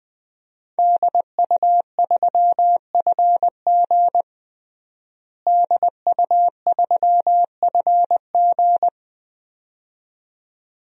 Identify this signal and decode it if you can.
Morse code, heard as a keyed tone, about 21 wpm, 700 Hz. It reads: DU3FG DU3FG
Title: DXCC-100 Call Signs - 5 Characters - Repeated 2x - 20wpm